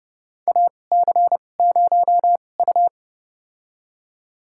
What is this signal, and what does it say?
Morse code, heard as a keyed tone, about 32 wpm, spaced as for 22 wpm, 700 Hz. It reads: AC0U
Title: USA Call Signs - 4 Characters - 2x Character Spacing - 30wpm